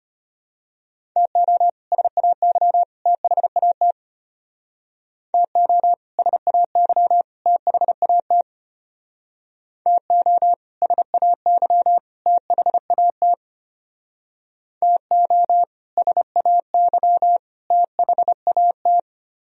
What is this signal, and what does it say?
Morse code, heard as a keyed tone, about 31 wpm, 700 Hz. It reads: TO SAY THAT TO SAY THAT TO SAY THAT TO SAY THAT
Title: Speed-Racing - Sets of 3 Words 25wpm